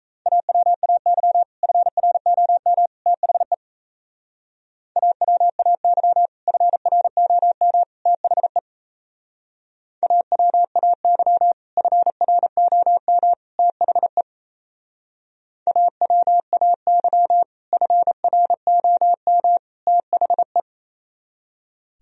Morse code, heard as a keyed tone, about 35 wpm, 700 Hz. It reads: AWAY FROM THE AWAY FROM THE AWAY FROM THE AWAY FROM THE